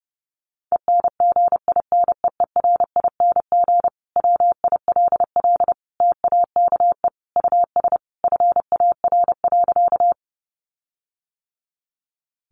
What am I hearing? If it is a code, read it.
Morse code, heard as a keyed tone, about 30 wpm, 700 Hz. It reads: ENGINEERING WILL TAKE US FAR.
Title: Sentences from Top 1500 Words 30wpm